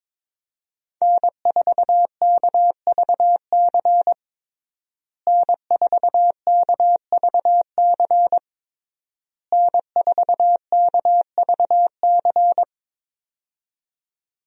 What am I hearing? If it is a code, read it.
Morse code, heard as a keyed tone, about 22 wpm, 700 Hz. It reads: N4KVC N4KVC N4KVC
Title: USA Call Signs - 5 Characters - Repeated 3x - 22wpm